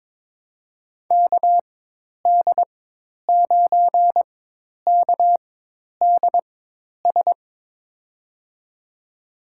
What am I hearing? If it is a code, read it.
Morse code, heard as a keyed tone, about 22 wpm, spaced as for 10 wpm, 700 Hz. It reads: KD9KDS
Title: USA Call Signs - 6 Characters - 4x Character Spacing - 22wpm